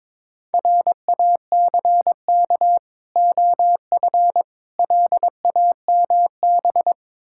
Morse code, heard as a keyed tone, about 22 wpm, 690 Hz. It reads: RACK OF LAMB
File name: Difficult Words with Hints - 22wpm